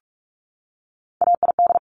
Morse code, heard as a keyed tone, about 46 wpm, 700 Hz. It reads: AID
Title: Top Three Letter Combinations in Words 45wpm